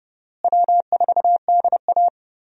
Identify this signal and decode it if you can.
Morse code, heard as a keyed tone, about 30 wpm, 700 Hz. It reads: W4DA
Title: USA Call Signs - 4 Characters - Repeated 2x - 30wpm